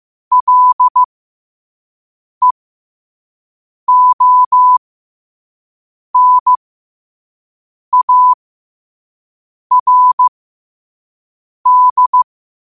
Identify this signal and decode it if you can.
Morse code, heard as a keyed tone, about 15 wpm, spaced as for 5 wpm, 1000 Hz. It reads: LEONARD
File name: qotdfiles / 5wpm